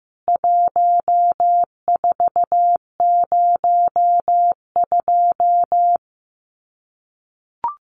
Morse code, heard as a keyed tone, about 15 wpm, 700 Hz. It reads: 1402
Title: CWT Exchange 15wpm